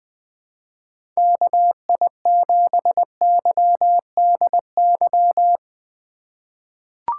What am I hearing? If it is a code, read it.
Morse code, heard as a keyed tone, about 20 wpm, 700 Hz. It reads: KI7YDY
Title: USA Call Signs - 3x Character Spacing - 20wpm